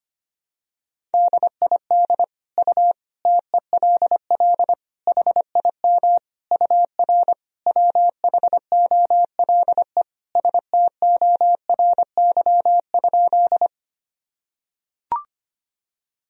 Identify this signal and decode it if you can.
Morse code, heard as a keyed tone, about 25 wpm, 700 Hz. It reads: DID U TELL HIM UR WHOLE STORY?
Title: Sentences from Top 400 Words and Abbreviations 25wpm